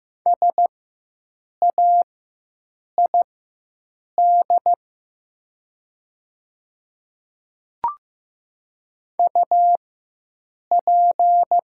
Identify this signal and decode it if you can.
Morse code, heard as a keyed tone, about 15 wpm, spaced as for 7 wpm, 700 Hz. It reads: SAID UP
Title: 4x Character Spacing - Top 100 Words - 15wpm